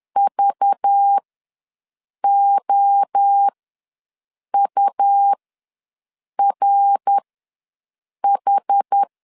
Morse code, heard as a keyed tone, about 10 wpm, spaced as for 6 wpm, 800 Hz. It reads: VOURH
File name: LEZIONI / 9 28 ws 10wpm 4124